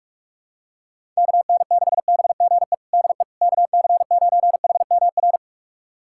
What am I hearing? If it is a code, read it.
Morse code, heard as a keyed tone, about 42 wpm, 700 Hz. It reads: KN6BGE DE KC9HMR
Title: USA Callsign DE Callsign 45wpm